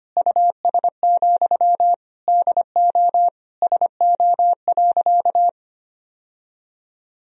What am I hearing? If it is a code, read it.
Morse code, heard as a keyed tone, about 25 wpm, 700 Hz. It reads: US, DO SO.
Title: Sentences from Top 1000 Words - Review Part 2 - 25wpm